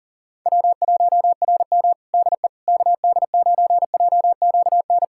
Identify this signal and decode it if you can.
Morse code, heard as a keyed tone, about 40 wpm, 700 Hz. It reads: W1RM DE KD9JQN
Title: K1USN SST QSO Practice 40wpm